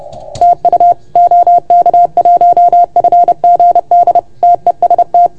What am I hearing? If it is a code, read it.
Morse code, heard as a keyed tone, about 30 wpm, 690 Hz. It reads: TU OK1FGD TEST
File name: Archiv / CUC381 / OK1FGD CUC 381